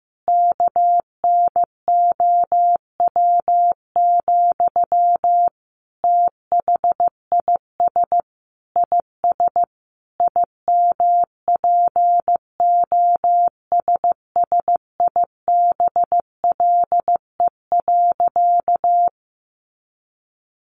Morse code, heard as a keyed tone, about 14 wpm, 700 Hz. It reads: KNOW, THIS IS IMPOSSIBLE.
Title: Sentences from Top 1100 Words 15wpm